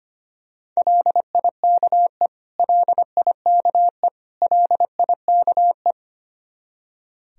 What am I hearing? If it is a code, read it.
Morse code, heard as a keyed tone, about 25 wpm, 700 Hz. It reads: LIKE LIKE LIKE
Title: Top 100 Words Repeated 25wpm